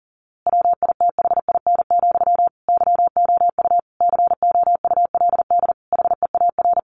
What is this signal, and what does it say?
Morse code, heard as a keyed tone, about 40 wpm, 700 Hz. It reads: WITHIN, YOU COULD HEAR
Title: A Sound Of Thunder 40wpm